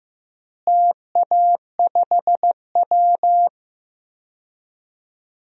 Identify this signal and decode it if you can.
Morse code, heard as a keyed tone, about 15 wpm, 700 Hz. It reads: TA5W
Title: DXCC-100 Call Signs - 4 Characters - Repeated 2x - 15wpm